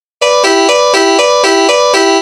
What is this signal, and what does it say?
There is no code.